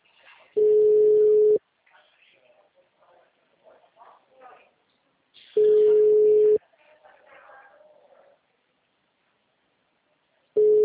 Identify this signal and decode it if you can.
Ringback tone (Europe, 425 Hz), with speech.